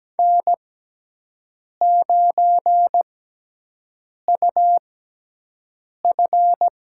Morse code, heard as a keyed tone, about 17 wpm, spaced as for 6 wpm, 700 Hz. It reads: N9UF
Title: USA Call Signs - 6x Character Spacing - 17wpm